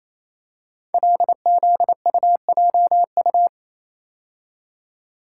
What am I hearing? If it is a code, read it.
Morse code, heard as a keyed tone, about 29 wpm, 690 Hz. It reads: LZUJU
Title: Code Groups - 5 Letters - 28wpm